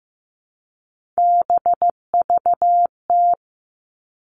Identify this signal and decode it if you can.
Morse code, heard as a keyed tone, about 15 wpm, 700 Hz. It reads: BVT
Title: All Cut Numbers - 3-Digit 15wpm